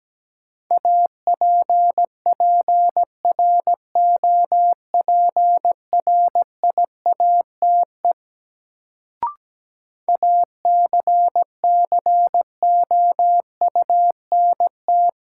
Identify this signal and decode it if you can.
Morse code, heard as a keyed tone, about 17 wpm, 700 Hz. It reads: APPROPRIATE ACCOUNT
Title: Sentences from Top 900 Words - New Word Drill - 17wpm